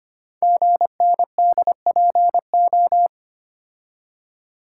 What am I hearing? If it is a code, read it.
Morse code, heard as a keyed tone, about 25 wpm, 700 Hz. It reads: GNDPO
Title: Code Groups - 5 Letters - 25wpm